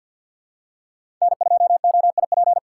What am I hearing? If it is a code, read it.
Morse code, heard as a keyed tone, about 51 wpm, 700 Hz. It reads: N2OIP
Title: USA Call Signs - 5 Characters - 4x Character Spacing - 50wpm